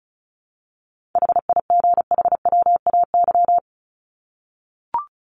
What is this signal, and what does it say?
Morse code, heard as a keyed tone, about 34 wpm, 700 Hz. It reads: HIGHWAY
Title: Top 1251-1500 Words 35wpm